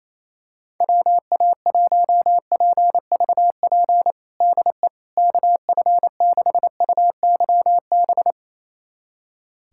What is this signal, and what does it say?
Morse code, heard as a keyed tone, about 28 wpm, 700 Hz. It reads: WA1PVP DE KF6UYB
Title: USA Callsign DE Callsign - 6 Characters - 28wpm